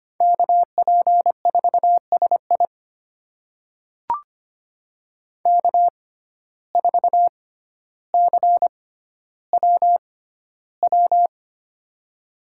Morse code, heard as a keyed tone, about 25 wpm, spaced as for 13 wpm, 700 Hz. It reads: KP4SI K4CWW
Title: USA Call Signs - 5 Characters - 6x Character Spacing - 25wpm